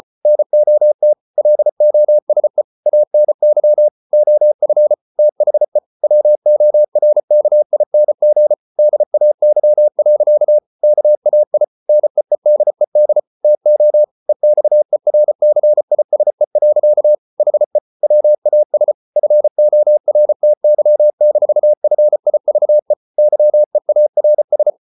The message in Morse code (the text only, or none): NOT LOSE ANY OF THE WORKING DAY. KAI NEEDED TO EXERCISE. HE WAS FORTY-FIVE YEARS